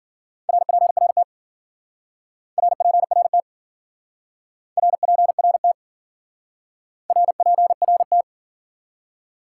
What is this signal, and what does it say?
Morse code, heard as a keyed tone, about 52 wpm, 700 Hz. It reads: RPRT RPRT RPRT RPRT